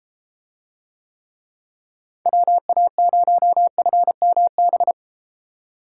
Morse code, heard as a keyed tone, about 33 wpm, 700 Hz. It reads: WA0FMB